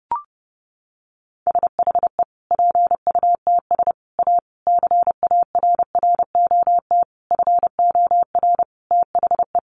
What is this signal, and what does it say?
Morse code, heard as a keyed tone, about 30 wpm, 700 Hz. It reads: SHE PUTS A CARROT FOR THE
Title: Easy Phrases 30wpm